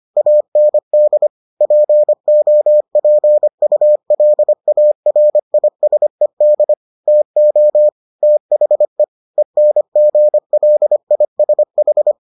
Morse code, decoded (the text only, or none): AND POPULARISED TO THE ENGLISH